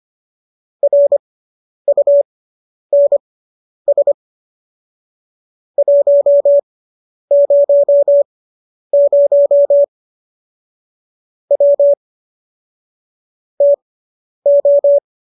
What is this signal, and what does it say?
Morse code, heard as a keyed tone, about 25 wpm, spaced as for 10 wpm, 570 Hz. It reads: RUNS 100 W TO